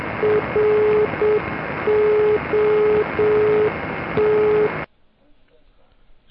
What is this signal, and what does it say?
Morse code, heard as a keyed tone, about 7 wpm, 420 Hz, with speech and radio noise. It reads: ROT